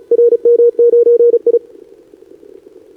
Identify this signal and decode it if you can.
Morse code, heard as a keyed tone, about 36 wpm, 470 Hz. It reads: RM9I
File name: CQWWCW2025 / NedeleRano 15m Scan